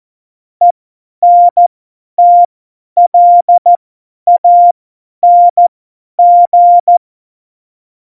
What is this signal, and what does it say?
Morse code, heard as a keyed tone, about 14 wpm, spaced as for 10 wpm, 700 Hz. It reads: ENTLANG
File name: Morsekurs / Witze / Witz 11. Dezember, 50 F70